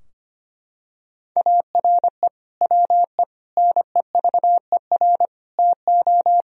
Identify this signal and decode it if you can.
Morse code, heard as a keyed tone, about 25 wpm, 700 Hz. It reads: ARE WE NEVER TO
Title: Sentences from Top 1000 Words - Review Part 3 - 25wpm